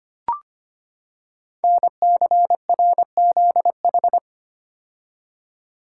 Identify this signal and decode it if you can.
Morse code, heard as a keyed tone, about 25 wpm, 700 Hz. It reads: NCRZH
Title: Code Groups - 5 Letters - 25wpm